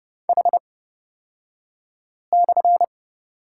Morse code, heard as a keyed tone, about 30 wpm, spaced as for 5 wpm, 690 Hz. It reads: H/
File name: Lesson 36d - BT - Two Characters 30wpm